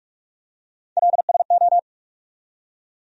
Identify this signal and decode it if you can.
Morse code, heard as a keyed tone, about 44 wpm, 700 Hz. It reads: LSO
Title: USA Call Sign Postfixes - 3 Letters 45wpm